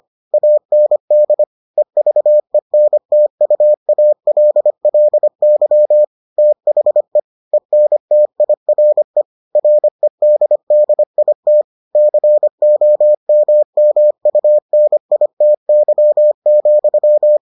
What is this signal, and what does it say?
Morse code, heard as a keyed tone, about 25 wpm, 600 Hz. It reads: AND EVENTUALLY THE ENTIRE REDDIT COMMUNITY,